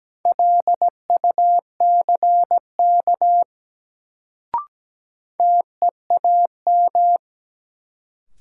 Morse code, heard as a keyed tone, about 17 wpm, 700 Hz. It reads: LUCK TEAM
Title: Sentences from Top 1500 Words - New Word Drill - 17wpm